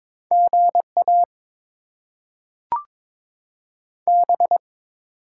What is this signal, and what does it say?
Morse code, heard as a keyed tone, about 22 wpm, 700 Hz. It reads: GA B
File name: Top 20 QSO Elements - 8x Character Spacing - 22wpm